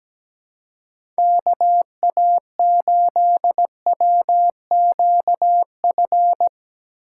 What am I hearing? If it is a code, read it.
Morse code, heard as a keyed tone, about 17 wpm, 700 Hz. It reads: KA8WQF
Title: USA Call Signs - 6x Character Spacing - 17wpm